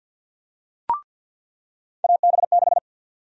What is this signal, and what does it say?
Morse code, heard as a keyed tone, about 48 wpm, 700 Hz. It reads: AB6
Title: USA Call Sign Prefixes - 50wpm